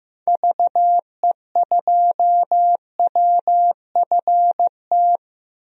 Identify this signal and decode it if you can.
Morse code, heard as a keyed tone, about 15 wpm, 700 Hz. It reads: VE2WFT
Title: DXCC-100 Call Signs - 6 Characters - Repeated 3x - 15wpm